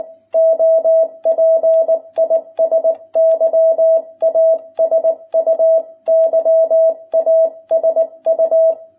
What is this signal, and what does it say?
Morse code, heard as a keyed tone, about 19 wpm, 660 Hz. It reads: OP IS YASU YASU